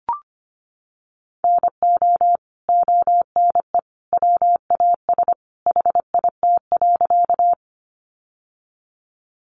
Morse code, heard as a keyed tone, about 25 wpm, 700 Hz. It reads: NO ONE WAS HIT.
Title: Sentences from Top 1200 Words 25wpm